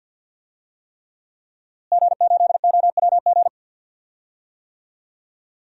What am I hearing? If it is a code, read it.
Morse code, heard as a keyed tone, about 52 wpm, 700 Hz. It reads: G8OWG
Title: DXCC-100 Call Signs - 5 Characters - Repeated 3x - 50wpm